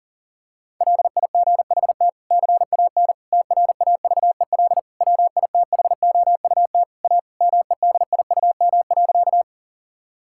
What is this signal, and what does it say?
Morse code, heard as a keyed tone, about 40 wpm, 700 Hz. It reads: LIGHT CAN TRAVEL WITHOUT A MEDIUM.
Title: Sentences from Top 1000 Words - Review Part 6 - 40wpm